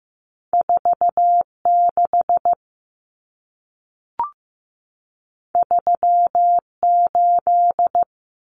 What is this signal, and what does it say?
Morse code, heard as a keyed tone, about 15 wpm, 700 Hz. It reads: 4 6 3 8
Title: Numbers - 2-Digit 15wpm